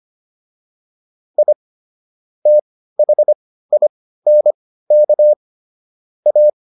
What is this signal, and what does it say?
Morse code, heard as a keyed tone, about 25 wpm, spaced as for 15 wpm, 600 Hz. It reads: I THINK A
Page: intermediate-practice-files